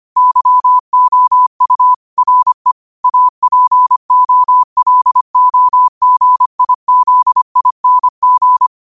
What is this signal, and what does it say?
Morse code, heard as a keyed tone, about 25 wpm, 1000 Hz. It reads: YOU RE APOLOGIZING
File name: qotdfiles / 25wpm